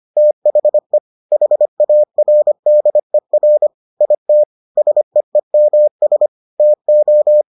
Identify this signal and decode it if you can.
Morse code, heard as a keyed tone, about 25 wpm, 600 Hz. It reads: THE HARDER IT SEEMS TO